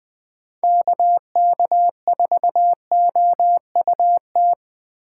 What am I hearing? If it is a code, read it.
Morse code, heard as a keyed tone, about 20 wpm, 700 Hz. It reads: KK4OUT